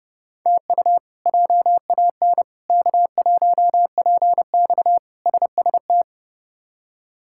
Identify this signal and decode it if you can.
Morse code, heard as a keyed tone, about 32 wpm, 700 Hz. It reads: TU JAN K1PX SST
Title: K1USN SST QSO Practice - 30wpm